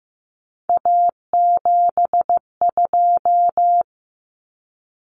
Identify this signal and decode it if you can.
Morse code, heard as a keyed tone, about 15 wpm, 700 Hz. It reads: A72
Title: Common Cut Numbers - 3-Digit 15wpm